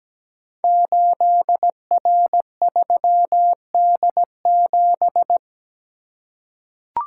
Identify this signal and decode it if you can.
Morse code, heard as a keyed tone, about 17 wpm, 700 Hz. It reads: 8R3D7
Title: Code Groups - 5 Letter:Numbers - 17wpm